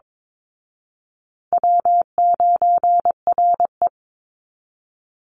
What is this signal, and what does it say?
Morse code, heard as a keyed tone, about 22 wpm, 700 Hz. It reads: W9RE
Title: Easy Call Signs Repeated 3x - 22wpm